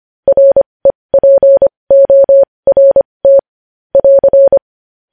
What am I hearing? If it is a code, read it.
Morse code, heard as a keyed tone, about 25 wpm, spaced as for 20 wpm, 550 Hz. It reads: REPORT <AR>
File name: podcast / News headlines-1x-25x20